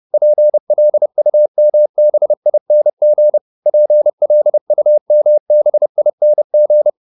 Morse code, heard as a keyed tone, about 30 wpm, 600 Hz. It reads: PLUMBING PLUMBING
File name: ING8 30